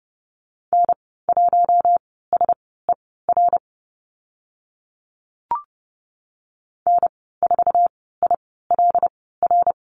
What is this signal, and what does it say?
Morse code, heard as a keyed tone, about 30 wpm, spaced as for 17 wpm, 700 Hz. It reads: N1SER N4ILR
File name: Lesson 19g - Letter w - Call Signs 30wpm